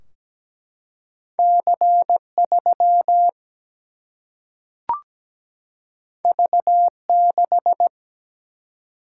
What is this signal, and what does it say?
Morse code, heard as a keyed tone, about 17 wpm, 700 Hz. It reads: C3 V6
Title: Single-Letter Single-Number 17wpm